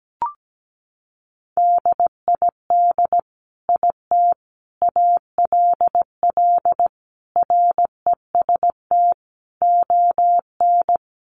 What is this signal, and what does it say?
Morse code, heard as a keyed tone, about 17 wpm, 700 Hz. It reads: DID IT ALL REST ON
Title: Sentences from Top 500 Words 17wpm